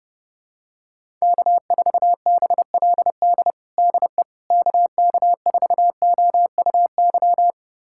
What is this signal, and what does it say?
Morse code, heard as a keyed tone, about 30 wpm, 700 Hz. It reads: K4BLD DE KK4OUY
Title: USA Callsign DE Callsign 30wpm